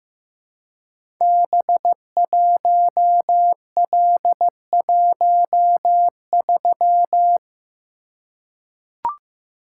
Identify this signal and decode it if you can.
Morse code, heard as a keyed tone, about 15 wpm, 700 Hz. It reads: B1L13
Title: Code Groups - 5 Letter:Numbers - 15wpm